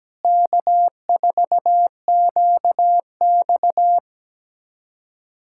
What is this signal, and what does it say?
Morse code, heard as a keyed tone, about 17 wpm, 700 Hz. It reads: K4QX
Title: USA Callsign DE Callsign - 4 Characters - 17wpm